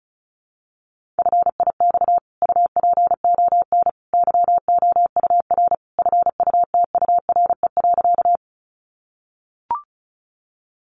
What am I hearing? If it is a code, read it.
Morse code, heard as a keyed tone, about 34 wpm, 700 Hz. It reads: FIX UPON YOUR FUTURE.
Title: Sentences from Top 1400 Words 35wpm